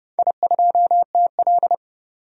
Morse code, heard as a keyed tone, about 30 wpm, 700 Hz. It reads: I2TL
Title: DXCC-100 Call Signs - 4 Characters - 30wpm